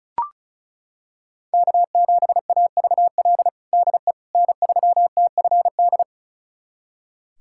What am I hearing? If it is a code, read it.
Morse code, heard as a keyed tone, about 36 wpm, 700 Hz. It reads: K7AVL DE N3TFD
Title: USA Callsign DE Callsign - 5 Characters - 35wpm